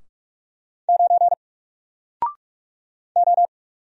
Morse code, heard as a keyed tone, about 46 wpm, 700 Hz. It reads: 9 O
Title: Single Letter-Number Alt 45wpm